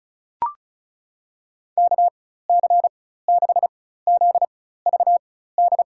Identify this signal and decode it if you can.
Morse code, heard as a keyed tone, about 34 wpm, spaced as for 16 wpm, 700 Hz. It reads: KC6ZVD
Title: USA Call Signs - 6 Characters - 4x Character Spacing - 35wpm